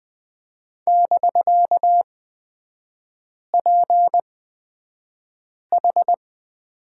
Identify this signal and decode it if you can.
Morse code, heard as a keyed tone, about 20 wpm, spaced as for 5 wpm, 700 Hz. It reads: <BK>PH
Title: 250 Lesson 40e - BK - Three Characters 20wpm